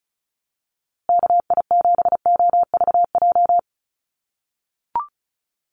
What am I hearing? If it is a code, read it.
Morse code, heard as a keyed tone, about 34 wpm, 700 Hz. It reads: KI7OVJ
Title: USA Call Signs Repeated 3x 35wpm